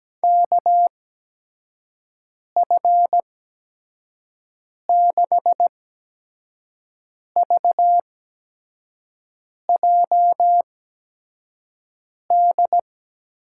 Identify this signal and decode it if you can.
Morse code, heard as a keyed tone, about 17 wpm, spaced as for 5 wpm, 700 Hz. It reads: KF6VJD
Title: USA Call Signs - 6 Characters - 8x Character Spacing - 17wpm